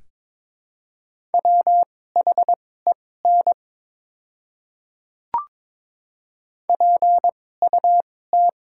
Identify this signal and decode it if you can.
Morse code, heard as a keyed tone, about 22 wpm, spaced as for 16 wpm, 700 Hz. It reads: WHEN PUT
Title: 2x Character Spacing - Top 200 Words - 22wpm